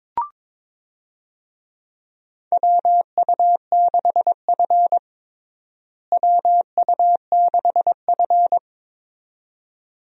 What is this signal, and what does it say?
Morse code, heard as a keyed tone, about 22 wpm, 700 Hz. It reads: WU6F WU6F
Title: USA Call Signs - 4 Characters - Repeated 2x - 22wpm